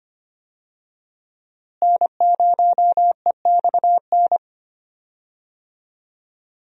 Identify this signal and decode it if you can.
Morse code, heard as a keyed tone, about 25 wpm, 700 Hz. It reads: N0EXN
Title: USA Call Signs - 5 Characters - Repeated 2x - 25wpm